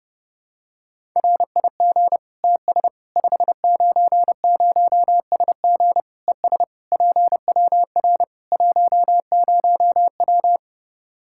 Morse code, heard as a keyed tone, about 30 wpm, 700 Hz. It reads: RIG TS 590SG ES PWR 10W
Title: Practice QSO - Exchange 2 - Info - 30wpm